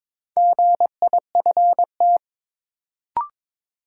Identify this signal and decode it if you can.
Morse code, heard as a keyed tone, about 21 wpm, 700 Hz. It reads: GIFT
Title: Sentences from Top 900 Words - New Word Drill - 22wpm